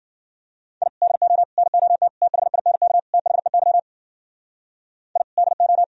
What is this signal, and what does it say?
Morse code, heard as a keyed tone, about 50 wpm, 700 Hz. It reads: I DO NOT THINK THEY I DO